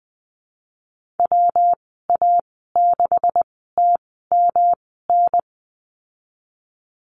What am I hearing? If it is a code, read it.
Morse code, heard as a keyed tone, about 20 wpm, spaced as for 15 wpm, 700 Hz. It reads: WA6TMN
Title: 128 Lesson 21g - Number 6 - Call Signs 20wpm